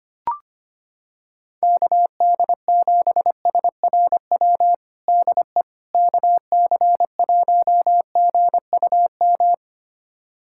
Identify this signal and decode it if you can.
Morse code, heard as a keyed tone, about 25 wpm, 700 Hz. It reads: KD7SRW DE KC1GUM
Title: USA Callsign DE Callsign - 6 Characters - 25wpm